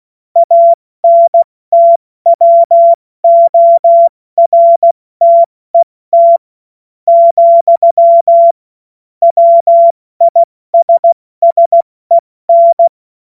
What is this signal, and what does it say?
Morse code, heard as a keyed tone, about 16 wpm, spaced as for 13 wpm, 680 Hz. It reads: ANTWORTET, WISSEN
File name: Morsekurs / Witze / Witz 05. Oktober 2021, 70 F80